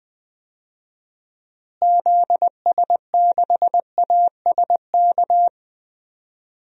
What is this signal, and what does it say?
Morse code, heard as a keyed tone, about 21 wpm, 700 Hz. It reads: ZS6ASK
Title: DXCC-100 Call Signs - 6 Characters - Repeated 3x - 20wpm